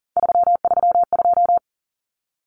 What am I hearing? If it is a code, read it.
Morse code, heard as a keyed tone, about 40 wpm, 700 Hz. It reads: 332
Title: Common Cut Numbers - 3-Digit 40wpm